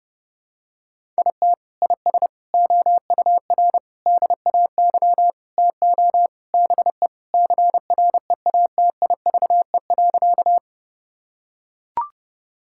Morse code, heard as a keyed tone, about 32 wpm, 700 Hz. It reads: IT IS OUR DAY TO BE CREATIVE.
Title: Sentences from Top 1000 Words - Review Part 5 - 30wpm